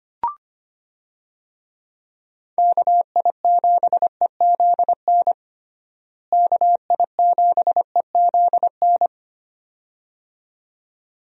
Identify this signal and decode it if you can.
Morse code, heard as a keyed tone, about 25 wpm, 700 Hz. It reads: KI7EZN KI7EZN
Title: USA Call Signs - 6 Characters - Repeated 2x - 25wpm